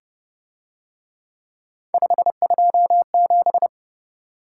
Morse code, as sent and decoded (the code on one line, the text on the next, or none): ..... ..--- --...
527